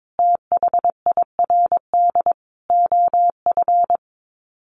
Morse code, heard as a keyed tone, about 22 wpm, 700 Hz. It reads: THIRD OF
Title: Sets of 5 Words Encore2 22wpm